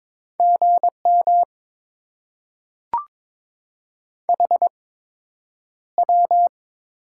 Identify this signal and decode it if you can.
Morse code, heard as a keyed tone, about 22 wpm, spaced as for 9 wpm, 700 Hz. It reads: GM HW
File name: Top 90 QSO Elements - 8x Character Spacing - 22wpm